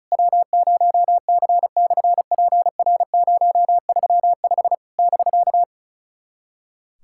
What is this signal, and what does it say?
Morse code, heard as a keyed tone, about 36 wpm, 690 Hz. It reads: W0C/PR035 <BK>
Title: SOTA QSO Practice 35wpm